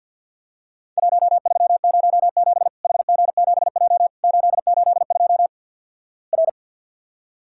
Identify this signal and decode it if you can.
Morse code, heard as a keyed tone, about 50 wpm, 700 Hz. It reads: 1307 HG7J 882 R
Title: Worked All Europe DX Contest - QTC Exchange - 50wpm